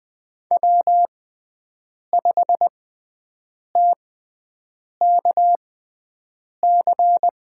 Morse code, heard as a keyed tone, about 20 wpm, spaced as for 7 wpm, 700 Hz. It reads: W5TKC
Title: USA Call Signs - 5 Characters - 6x Character Spacing - 20wpm